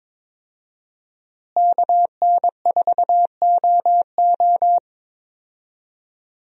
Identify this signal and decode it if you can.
Morse code, heard as a keyed tone, about 22 wpm, 700 Hz. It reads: KN4OO
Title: USA Call Signs - 5 Characters - Repeated 3x - 22wpm